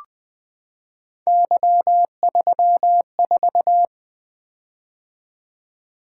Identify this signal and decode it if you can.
Morse code, heard as a keyed tone, about 20 wpm, 700 Hz. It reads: Y34
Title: Three Letter-Number Combinations - 20wpm